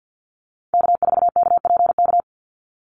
Morse code, heard as a keyed tone, about 50 wpm, 700 Hz. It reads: K4KPK
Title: NA SOTA Activator Callsigns 50wpm